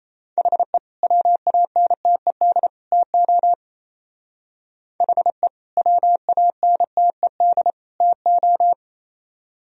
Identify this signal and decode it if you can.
Morse code, heard as a keyed tone, about 35 wpm, 700 Hz. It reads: HE WANTED TO HE WANTED TO